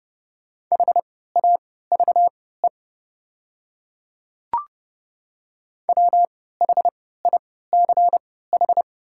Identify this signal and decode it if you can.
Morse code, heard as a keyed tone, about 30 wpm, spaced as for 17 wpm, 700 Hz. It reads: HAVE WHICH
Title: 3x Character Spacing - Top 100 Words - 30wpm